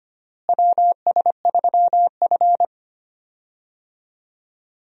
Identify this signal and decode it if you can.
Morse code, heard as a keyed tone, about 25 wpm, 700 Hz. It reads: WS3F
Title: USA Call Signs - 4 Characters - Repeated 2x - 25wpm